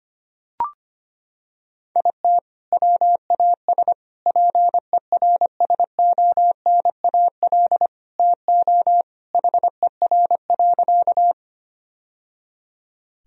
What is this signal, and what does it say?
Morse code, heard as a keyed tone, about 25 wpm, 700 Hz. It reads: IT WAS PERSONAL TO HER.
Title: Sentences from Top 1000 Words - Review Part 4 - 25wpm